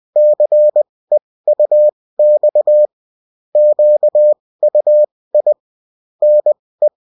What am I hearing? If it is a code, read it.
Morse code, heard as a keyed tone, about 20 wpm, spaced as for 16 wpm, 600 Hz. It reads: CEUX QUI NE